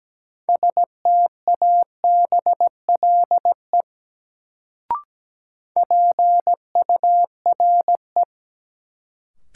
Morse code, heard as a keyed tone, about 17 wpm, 700 Hz. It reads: STABLE PURE
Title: Sentences from Top 1500 Words - New Word Drill Alt - 17wpm